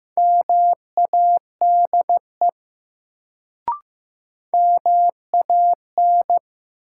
Morse code, heard as a keyed tone, about 15 wpm, 700 Hz. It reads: MADE MAN
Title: Sentences from Top 200 Words - New Word Drill - 15wpm